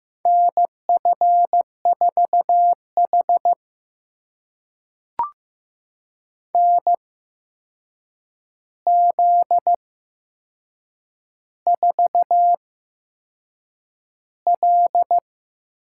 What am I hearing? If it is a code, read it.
Morse code, heard as a keyed tone, about 15 wpm, spaced as for 6 wpm, 700 Hz. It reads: NF4H NZ4L
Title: USA Call Signs - 4 Characters - 8x Character Spacing - 15wpm